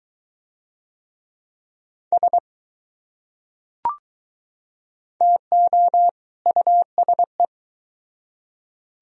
Morse code, heard as a keyed tone, about 21 wpm, 700 Hz. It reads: S TO USE